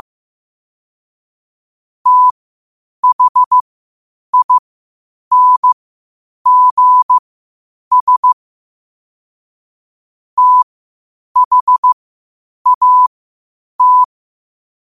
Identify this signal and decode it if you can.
Morse code, heard as a keyed tone, about 15 wpm, spaced as for 8 wpm, 1000 Hz. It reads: THINGS THAT